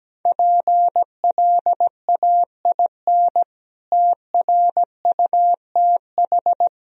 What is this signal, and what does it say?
Morse code, heard as a keyed tone, about 17 wpm, 700 Hz. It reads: PLAIN TRUTH
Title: Sentences from Top 1000 Words - Review Part 2 - 17wpm